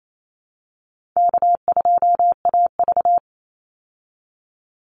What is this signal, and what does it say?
Morse code, heard as a keyed tone, about 28 wpm, 700 Hz. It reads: K2AV
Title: Easy Call Signs Repeated 2x - 28wpm